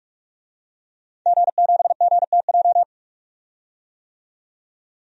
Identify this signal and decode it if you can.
Morse code, heard as a keyed tone, about 46 wpm, 700 Hz. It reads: G7GTJ
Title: DXCC-100 Call Signs - 5 Characters - 45wpm